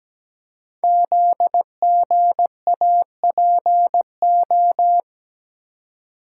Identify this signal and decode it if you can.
Morse code, heard as a keyed tone, about 17 wpm, 700 Hz. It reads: ZGAPO
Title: Code Groups - 5 Letters - 17wpm